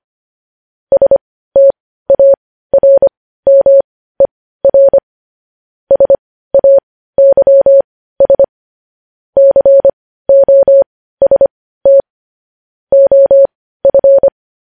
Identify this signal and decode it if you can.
Morse code, heard as a keyed tone, about 25 wpm, spaced as for 15 wpm, 550 Hz. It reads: STARMER SAYS COST OF
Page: news-headlines-in-morse-code-at-15-wpm